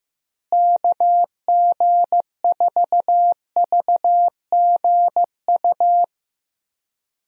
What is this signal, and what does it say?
Morse code, heard as a keyed tone, about 15 wpm, 700 Hz. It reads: KG4VGU
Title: USA Call Signs - 6 Characters - Repeated 2x - 15wpm